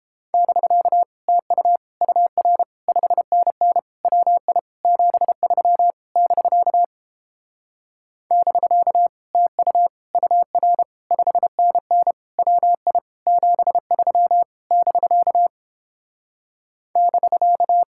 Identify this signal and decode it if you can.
Morse code, heard as a keyed tone, about 29 wpm, 700 Hz. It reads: <BK> TU UR 5NN WI 73 <BK> <BK> TU UR 5NN WI 73 <BK> <BK>